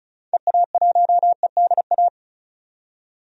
Morse code, heard as a keyed tone, about 36 wpm, 700 Hz. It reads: EA1EDA
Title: DXCC-100 Call Signs - 6 Characters - 35wpm